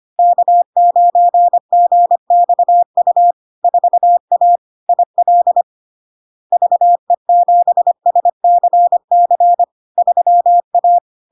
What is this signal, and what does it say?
Morse code, heard as a keyed tone, about 25 wpm, 700 Hz. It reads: K9GXU 4A IL VE7SCC 3A
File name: LARK-FD-2022-Log-in-CW-25wpm-700-Hz